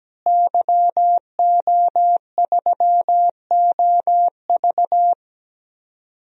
Morse code, heard as a keyed tone, about 17 wpm, 700 Hz. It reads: YO3OV
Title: DXCC-100 Call Signs - 5 Characters - Repeated 3x - 17wpm